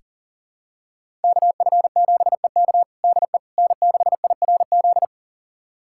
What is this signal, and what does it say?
Morse code, heard as a keyed tone, about 40 wpm, 700 Hz. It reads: KF7EK DE N6IRZ
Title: USA Callsign DE Callsign - 5 Characters - 40wpm